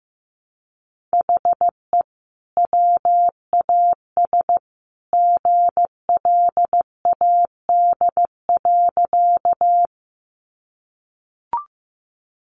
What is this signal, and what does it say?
Morse code, heard as a keyed tone, about 15 wpm, 700 Hz. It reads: HE WAS GLAD.
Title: Sentences from Top 800 Words 15wpm